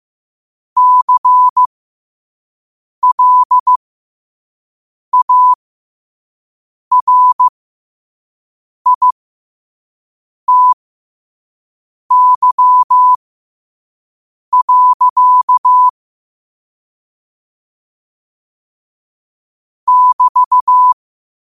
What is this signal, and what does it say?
Morse code, heard as a keyed tone, about 15 wpm, spaced as for 5 wpm, 1000 Hz. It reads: CLARITY. <BT>
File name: qotdfiles / 5wpm